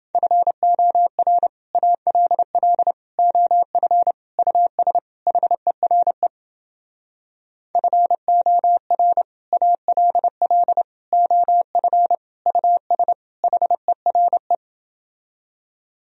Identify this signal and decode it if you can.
Morse code, heard as a keyed tone, about 29 wpm, 700 Hz. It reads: FOR ALL OF US HERE FOR ALL OF US HERE